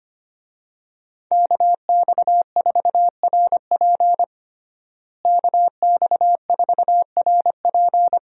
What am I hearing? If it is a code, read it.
Morse code, heard as a keyed tone, about 25 wpm, 700 Hz. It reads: KX4RP KX4RP
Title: USA Call Signs - 5 Characters - Repeated 2x - 25wpm